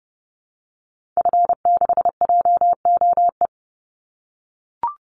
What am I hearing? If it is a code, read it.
Morse code, heard as a keyed tone, about 30 wpm, 700 Hz. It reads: F6JOE
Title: Call Signs - 30wpm